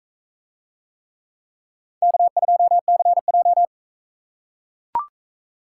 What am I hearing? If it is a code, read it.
Morse code, heard as a keyed tone, about 44 wpm, 700 Hz. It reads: K2CJ